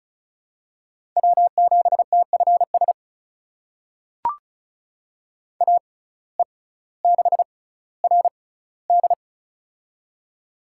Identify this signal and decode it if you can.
Morse code, heard as a keyed tone, about 36 wpm, spaced as for 18 wpm, 700 Hz. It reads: W7TFS AE6RD
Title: USA Call Signs - 5 Characters - 6x Character Spacing - 35wpm